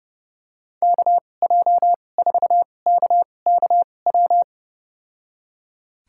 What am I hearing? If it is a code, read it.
Morse code, heard as a keyed tone, about 32 wpm, spaced as for 22 wpm, 700 Hz. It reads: KJ4KKW
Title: USA Call Signs - 2x Character Spacing - 30wpm